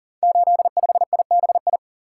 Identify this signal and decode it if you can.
Morse code, heard as a keyed tone, about 39 wpm, 700 Hz. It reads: 85IBI
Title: Code Groups - 5 Letter:Numbers - 40wpm